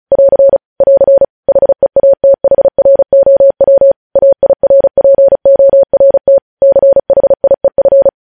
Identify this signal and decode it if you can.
Morse code, heard as a keyed tone, about 36 wpm, 550 Hz. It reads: <AR> <AR> HEATHROW AIRPORT CHIEF
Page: News Headlines in Morse Code at 35 WPM